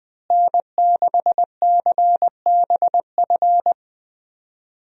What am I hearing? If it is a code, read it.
Morse code, heard as a keyed tone, about 20 wpm, 700 Hz. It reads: N6CBF
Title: USA Call Signs - 5 Characters - Repeated 2x - 20wpm